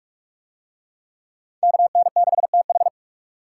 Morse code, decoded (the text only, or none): KN6TH